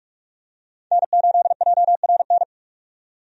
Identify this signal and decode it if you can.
Morse code, heard as a keyed tone, about 42 wpm, 700 Hz. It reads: N8JRN
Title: USA Call Signs - 5 Characters - 45wpm